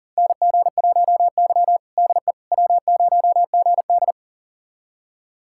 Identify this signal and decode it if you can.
Morse code, heard as a keyed tone, about 40 wpm, 700 Hz. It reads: NG1Y DE W0GD
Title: USA Callsign DE Callsign - 4 Characters - 40wpm